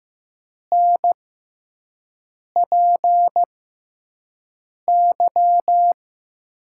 Morse code, heard as a keyed tone, about 15 wpm, spaced as for 5 wpm, 700 Hz. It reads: NPY
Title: USA Call Signs - 6 Characters - 6x Character Spacing - 15wpm